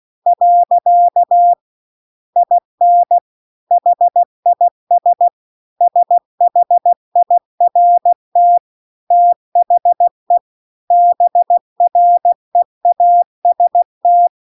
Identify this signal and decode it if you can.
Morse code, heard as a keyed tone, about 16 wpm, 700 Hz. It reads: . IN HIS SHIRT THE BREAST